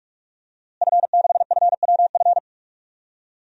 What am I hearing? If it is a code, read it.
Morse code, heard as a keyed tone, about 46 wpm, 700 Hz. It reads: F6FWF
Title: DXCC-100 Call Signs - 5 Characters - Repeated 2x - 45wpm